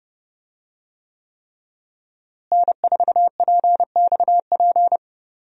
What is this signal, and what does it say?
Morse code, heard as a keyed tone, about 30 wpm, 700 Hz. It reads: N4PXP
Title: USA Call Signs - 5 Characters - Repeated 3x - 30wpm